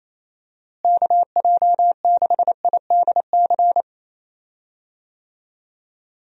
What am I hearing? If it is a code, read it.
Morse code, heard as a keyed tone, about 28 wpm, 700 Hz. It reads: KJ6IDC
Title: USA Call Signs - 6 Characters - Repeated 2x - 28wpm